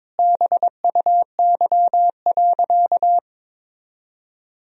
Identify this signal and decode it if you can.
Morse code, heard as a keyed tone, about 22 wpm, 700 Hz. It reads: BUY.
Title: Sentences from Top 1000 Words - Review Part 4 - 22wpm